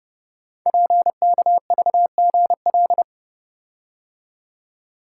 Morse code, heard as a keyed tone, about 30 wpm, 700 Hz. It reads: PKVGL
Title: Code Groups - 5 Letters - 30wpm